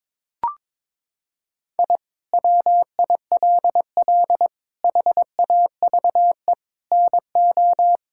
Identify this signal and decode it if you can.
Morse code, heard as a keyed tone, about 22 wpm, 700 Hz. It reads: I WILL HAVE NO
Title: Sentences from Top 1000 Words - Review Part 5 - 22wpm